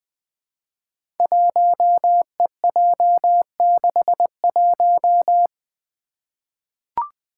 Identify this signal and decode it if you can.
Morse code, heard as a keyed tone, about 20 wpm, 700 Hz. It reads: 1EJ61
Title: Five Challenging Letter-Number Combinations 20wpm